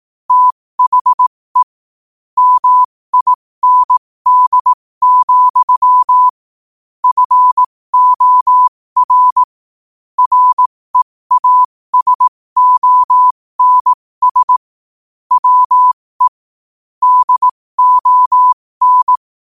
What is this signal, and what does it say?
Morse code, heard as a keyed tone, about 18 wpm, spaced as for 15 wpm, 1000 Hz. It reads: THE MIND, FOR REASONS WE DON
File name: qotdfiles / 15wpm